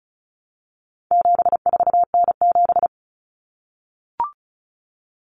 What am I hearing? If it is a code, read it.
Morse code, heard as a keyed tone, about 34 wpm, 700 Hz. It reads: 74N7
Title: Common Cut Numbers - 4-Digit 35wpm